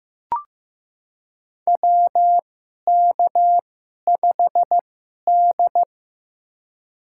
Morse code, heard as a keyed tone, about 15 wpm, spaced as for 11 wpm, 700 Hz. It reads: WK5D
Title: USA Call Signs - 2x Character Spacing - 15wpm